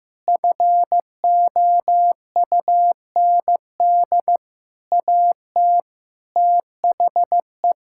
Morse code, heard as a keyed tone, about 15 wpm, 700 Hz. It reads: FOUND AT THE
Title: Sentences from Top 1600 Words 15wpm